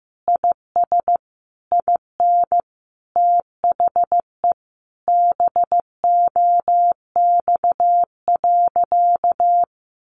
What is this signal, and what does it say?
Morse code, heard as a keyed tone, about 15 wpm, 700 Hz. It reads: IS IN THE BOX.
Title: Easy Phrases 15wpm